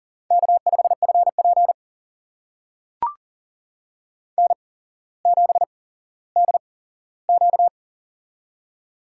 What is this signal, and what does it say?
Morse code, heard as a keyed tone, about 39 wpm, spaced as for 17 wpm, 700 Hz. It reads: K5FP N7DQ
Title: USA Call Signs - 8x Character Spacing - 40wpm